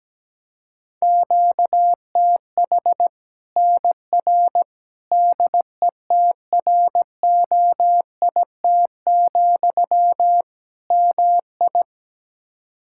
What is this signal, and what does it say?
Morse code, heard as a keyed tone, about 17 wpm, 700 Hz. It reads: QTH NR DETROIT, MI
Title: Practice QSO - Exchange 1 - Beginning - 17wpm